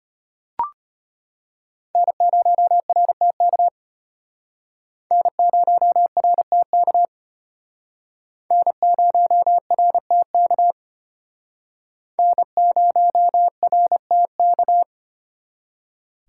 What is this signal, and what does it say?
Morse code, heard as a keyed tone, about 33 wpm, 700 Hz. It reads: N0RTK N0RTK N0RTK N0RTK